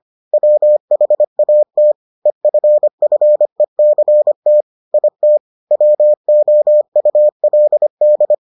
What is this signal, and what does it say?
Morse code, heard as a keyed tone, about 25 wpm, 600 Hz. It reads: WHAT EFFECT IT WOULD